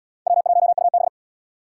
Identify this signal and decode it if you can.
Morse code, heard as a keyed tone, about 77 wpm, 700 Hz. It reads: W1UB